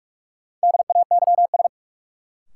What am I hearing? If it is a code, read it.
Morse code, heard as a keyed tone, about 43 wpm, 700 Hz. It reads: DAYS
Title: Sets of 5 Words Encore - New Word Drill - 45wpm